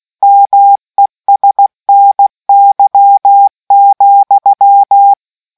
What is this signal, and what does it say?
Morse code, heard as a keyed tone, about 16 wpm, 800 Hz. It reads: MESNY,